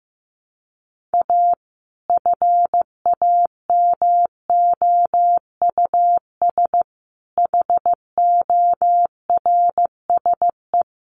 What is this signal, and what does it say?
Morse code, heard as a keyed tone, about 15 wpm, 700 Hz. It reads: A FAMOUS HORSE
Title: Sentences from Top 1500 Words 15wpm